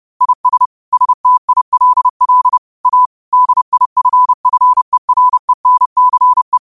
Morse code, heard as a keyed tone, about 30 wpm, 1000 Hz. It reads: IS STILL A DIFFERENCE
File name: 30wpm